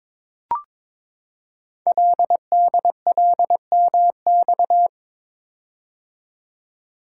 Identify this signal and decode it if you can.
Morse code, heard as a keyed tone, about 22 wpm, 690 Hz. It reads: LDLMX
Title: Code Groups - 5 Letters - 22wpm